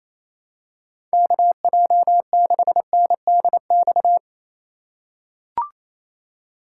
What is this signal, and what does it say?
Morse code, heard as a keyed tone, about 28 wpm, 700 Hz. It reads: KJ6NDX
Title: USA Call Signs - 6 Characters - 2x Character Spacing - 28wpm